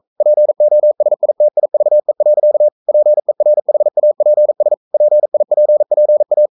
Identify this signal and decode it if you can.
Morse code, heard as a keyed tone, about 42 wpm, 600 Hz. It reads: POSITIVE. PERHAPS PIPPA
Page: Romanen Never af Ken Follet på engelsk.